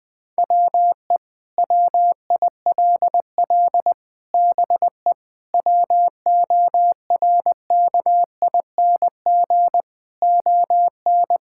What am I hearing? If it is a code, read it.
Morse code, heard as a keyed tone, about 20 wpm, 700 Hz. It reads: WE WILL BE WORKING ON
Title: Sentences with 6-Word Phrases - 20wpm